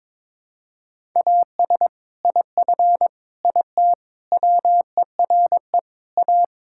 Morse code, heard as a keyed tone, about 22 wpm, 700 Hz. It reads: AS IF IT WERE A
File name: practice / 22wpm / Build A Phrase - 5-Word Phrases - 22wpm